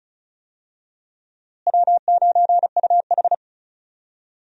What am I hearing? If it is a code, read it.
Morse code, heard as a keyed tone, about 34 wpm, 700 Hz. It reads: W9UH
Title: USA Call Signs - 4 Characters - Repeated 2x - 35wpm